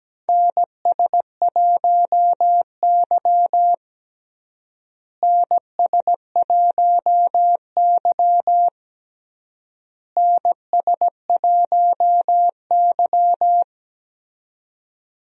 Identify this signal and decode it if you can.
Morse code, heard as a keyed tone, about 17 wpm, 700 Hz. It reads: NS1Y NS1Y NS1Y
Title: USA Call Signs - 4 Characters - Repeated 3x - 17wpm